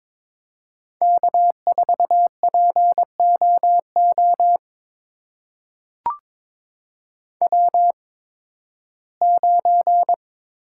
Morse code, heard as a keyed tone, about 22 wpm, spaced as for 14 wpm, 700 Hz. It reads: K4POO W9
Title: USA Call Signs - 5 Characters - 8x Character Spacing - 22wpm